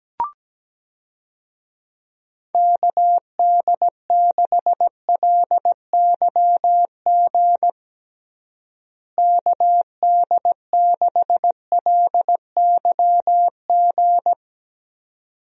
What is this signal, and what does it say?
Morse code, heard as a keyed tone, about 17 wpm, 700 Hz. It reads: KD6LYG KD6LYG
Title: USA Call Signs - 6 Characters - Repeated 2x - 17wpm